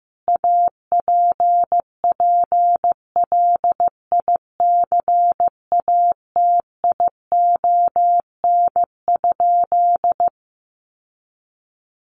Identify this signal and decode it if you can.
Morse code, heard as a keyed tone, about 15 wpm, 700 Hz. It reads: APPLICATION?
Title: Sentences from Top 1200 Words 15wpm